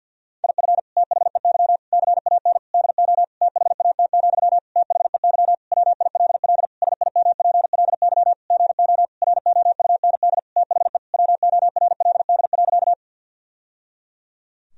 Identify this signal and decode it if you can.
Morse code, heard as a keyed tone, about 50 wpm, 700 Hz. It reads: IF THEY CAN DO THAT, THEY WILL SIMPLY GO ROUND THE WORLD.
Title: Sentences from Top 1000 Words - Review Part 4 - 50wpm